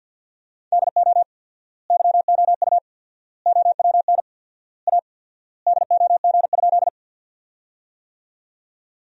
Morse code, heard as a keyed tone, about 50 wpm, 700 Hz. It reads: DO YOU OWN A DOG?
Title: 4x Word-Spacing - Sentences from Top 400 Words - 50wpm